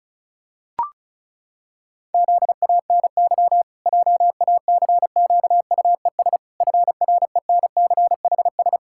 Morse code, heard as a keyed tone, about 36 wpm, 700 Hz. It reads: ZANY JACQUES FRENCHS
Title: Pangram Challenge v2 - 35wpm